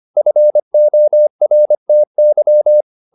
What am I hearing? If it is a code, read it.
Morse code, heard as a keyed tone, about 25 wpm, 600 Hz. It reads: FORTY